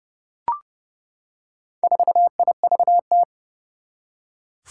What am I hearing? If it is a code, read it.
Morse code, heard as a keyed tone, about 30 wpm, 700 Hz. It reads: 4IVT
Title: Four Challenging Letter-Number Combinations 30wpm